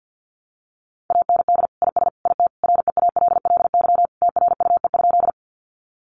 Morse code, heard as a keyed tone, about 48 wpm, 700 Hz. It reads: AND IS IT REALLY TRUE?
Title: Sentences from Top 600 Words 50wpm